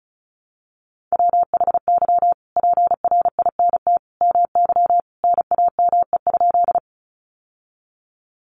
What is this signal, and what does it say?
Morse code, heard as a keyed tone, about 34 wpm, 700 Hz. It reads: WHY PRINT MY NAME?
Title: Sentences from Top 1400 Words 35wpm